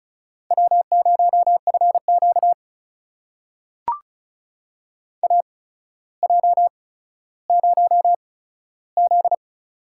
Morse code, heard as a keyed tone, about 34 wpm, spaced as for 15 wpm, 700 Hz. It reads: W0FQ AJ0Z
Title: USA Call Signs - 4 Characters - 8x Character Spacing - 35wpm